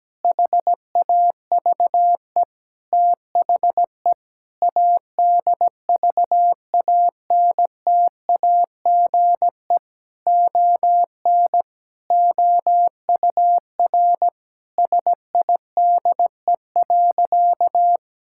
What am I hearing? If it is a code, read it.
Morse code, heard as a keyed tone, about 17 wpm, 700 Hz. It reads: HAVE THE ADVANTAGE ON OUR SIDE.
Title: Sentences from Top 1000 Words - Review Part 6 - 17wpm